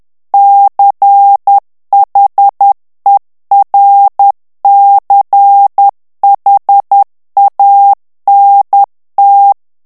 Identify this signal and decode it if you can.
Morse code, heard as a keyed tone, about 10 wpm, 800 Hz. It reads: CHERCHANT
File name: FRANCAIS TEXTE CLAIR 2 VITESSE 10 MOTS MINUTES